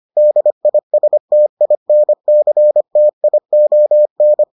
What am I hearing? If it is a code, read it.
Morse code, heard as a keyed tone, about 25 wpm, 600 Hz. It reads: DISTINCTION